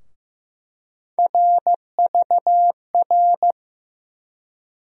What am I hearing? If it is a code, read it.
Morse code, heard as a keyed tone, about 15 wpm, 700 Hz. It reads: RVR
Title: USA Call Sign Postfixes - 3 Letters 15wpm